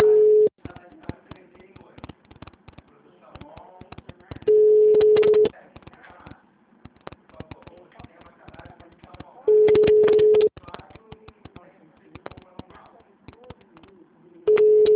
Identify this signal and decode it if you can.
Ringback tone (Europe, 425 Hz), with speech.
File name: out- -2003- - - .13